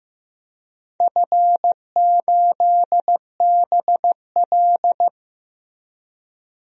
Morse code, heard as a keyed tone, about 15 wpm, 700 Hz. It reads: F8BL
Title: DXCC-100 Call Signs - 4 Characters - Repeated 2x - 15wpm